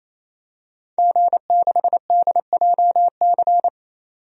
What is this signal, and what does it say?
Morse code, heard as a keyed tone, about 28 wpm, 690 Hz. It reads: G6DJC
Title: DXCC-100 Call Signs - 5 Characters - 28wpm